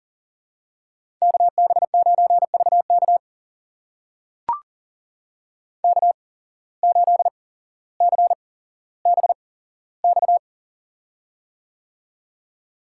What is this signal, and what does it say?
Morse code, heard as a keyed tone, about 42 wpm, spaced as for 17 wpm, 700 Hz. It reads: KB9VK K8CBX
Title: USA Call Signs - 5 Characters - 8x Character Spacing - 40wpm